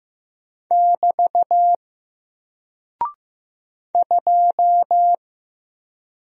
Morse code, heard as a keyed tone, about 15 wpm, 700 Hz. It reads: <BT> 2
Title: Single Character (Letter:Number:Punctuation:Prosign) Alt 15wpm